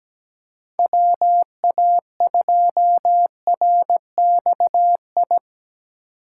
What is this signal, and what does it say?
Morse code, heard as a keyed tone, about 17 wpm, 700 Hz. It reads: WA2RXI
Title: USA Call Signs - 6 Characters - 6x Character Spacing - 17wpm